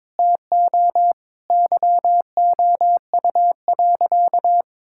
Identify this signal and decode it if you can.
Morse code, heard as a keyed tone, about 22 wpm, 700 Hz. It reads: TO YOU.
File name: practice / 22wpm / Sentences with 3-Word Phrases - 22wpm